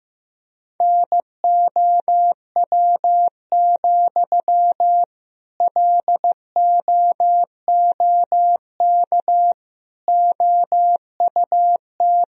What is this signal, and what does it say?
Morse code, heard as a keyed tone, about 15 wpm, 700 Hz. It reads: NOW, LOOK OUT
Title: Sentences from Top 1000 Words - Review Part 4 - 15wpm